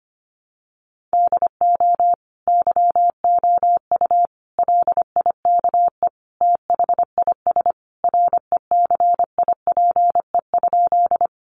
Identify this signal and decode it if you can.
Morse code, heard as a keyed tone, about 25 wpm, 700 Hz. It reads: DO YOU LIKE THIS RECIPE?
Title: Sentences from Top 1200 Words 25wpm